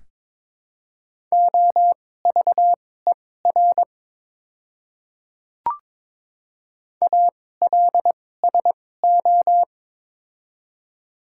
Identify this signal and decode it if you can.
Morse code, heard as a keyed tone, about 22 wpm, spaced as for 16 wpm, 700 Hz. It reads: OVER ALSO
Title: 2x Character Spacing - Top 200 Words - 22wpm